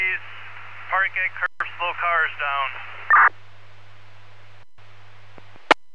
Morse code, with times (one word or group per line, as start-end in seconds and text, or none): none